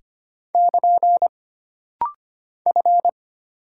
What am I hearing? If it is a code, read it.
Morse code, heard as a keyed tone, about 25 wpm, 700 Hz. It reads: <KN> F
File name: Single Character (Letter:Number:Punctuation:Prosign) - Rapid-Fire - 25wpm